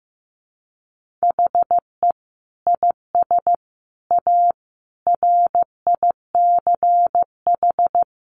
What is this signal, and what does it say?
Morse code, heard as a keyed tone, about 14 wpm, 700 Hz. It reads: HE IS A RICH
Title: Sentences from Top 1100 Words 15wpm